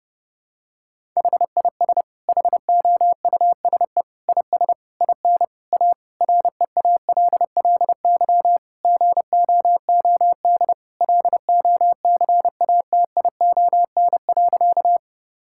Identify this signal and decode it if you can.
Morse code, heard as a keyed tone, about 30 wpm, 700 Hz. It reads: HIS HOUSE IS IN A REALLY GOOD LOCATION.
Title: Sentences from Top 1000 Words - Review Part 6 - 30wpm